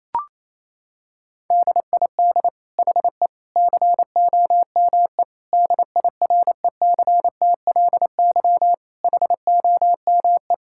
Morse code, heard as a keyed tone, about 28 wpm, 700 Hz. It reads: DID HE COME DIRECTLY HOME
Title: Sentences from Top 1000 Words - Review Part 5 - 28wpm